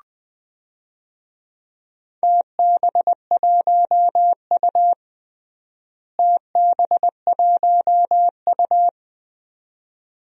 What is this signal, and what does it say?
Morse code, heard as a keyed tone, about 20 wpm, 700 Hz. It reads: TB1U TB1U
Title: DXCC-100 Call Signs - 4 Characters - Repeated 2x - 20wpm